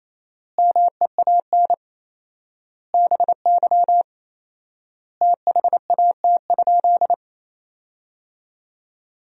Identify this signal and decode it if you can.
Morse code, heard as a keyed tone, about 28 wpm, 700 Hz. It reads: MEAN BY THAT?
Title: 4x Word-Spacing - Sentences from Top 200 Words - 28wpm